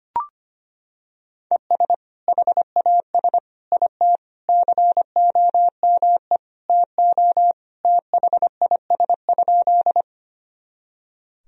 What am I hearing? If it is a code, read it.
Morse code, heard as a keyed tone, about 25 wpm, 700 Hz. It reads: ES HAS IT COME TO THIS?
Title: Sentences from Top 100 Words and Abbreviations 25wpm